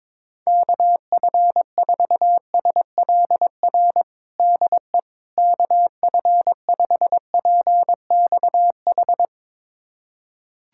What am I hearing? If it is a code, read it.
Morse code, heard as a keyed tone, about 22 wpm, 700 Hz. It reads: KF4SLR DE KF5PXH
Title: USA Callsign DE Callsign - 6 Characters - 22wpm